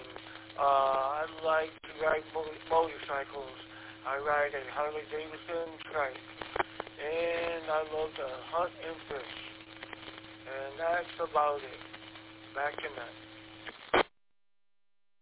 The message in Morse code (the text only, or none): none